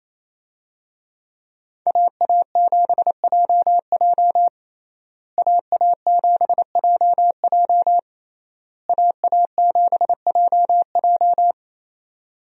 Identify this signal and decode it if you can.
Morse code, heard as a keyed tone, about 28 wpm, 700 Hz. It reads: AA7JJ AA7JJ AA7JJ
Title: USA Call Signs - 5 Characters - Repeated 3x - 28wpm